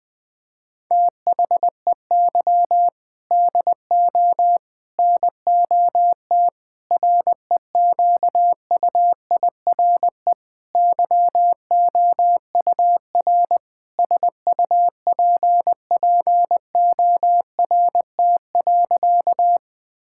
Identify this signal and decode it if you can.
Morse code, heard as a keyed tone, about 20 wpm, 700 Hz. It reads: THEY DO NOT REQUIRE YOUR SUPPORT.
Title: Sentences from Top 1000 Words - Review Part 6 - 20wpm